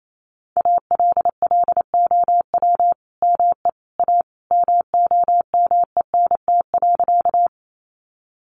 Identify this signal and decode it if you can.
Morse code, heard as a keyed tone, about 28 wpm, 700 Hz. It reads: ALLOW ME A MOMENT.
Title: Sentences from Top 1000 Words 28wpm